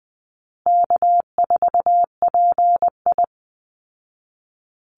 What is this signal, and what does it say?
Morse code, heard as a keyed tone, about 19 wpm, 700 Hz. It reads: K4PI
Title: Easy Call Signs Repeated 2x - 20wpm